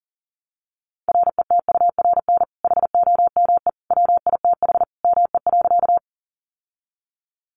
Morse code, heard as a keyed tone, about 37 wpm, 700 Hz. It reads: RETURN HOME WITH ME.
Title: Sentences from Top 800 Words 40wpm